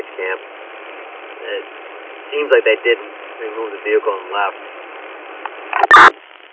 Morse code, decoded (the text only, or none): none